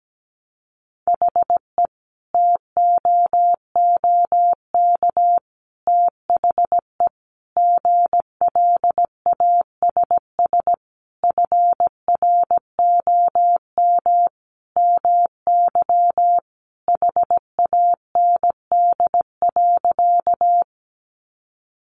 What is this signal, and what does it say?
Morse code, heard as a keyed tone, about 17 wpm, 700 Hz. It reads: HE TOOK THE GLASS FROM MY HAND.
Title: Sentences from Top 1300 Words 17wpm